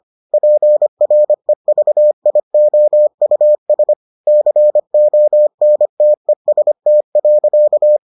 Morse code, heard as a keyed tone, about 25 wpm, 600 Hz. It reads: PREVIOUS CONTEST.